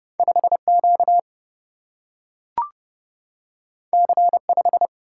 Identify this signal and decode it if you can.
Morse code, heard as a keyed tone, about 30 wpm, 700 Hz. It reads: 5 Q C 5
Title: Single-Letter Single-Number 30wpm